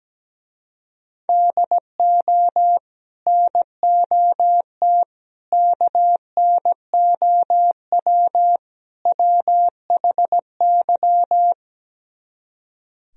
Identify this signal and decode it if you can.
Morse code, heard as a keyed tone, about 17 wpm, 700 Hz. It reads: DO NOT KNOW WHY
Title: Speed-Racing - Sets of 4 Words 17wpm